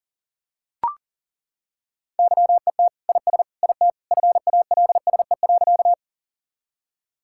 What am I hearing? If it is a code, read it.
Morse code, heard as a keyed tone, about 40 wpm, 700 Hz. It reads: YET IS IT FALSE.
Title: Sentences from Top 1600 Words 40wpm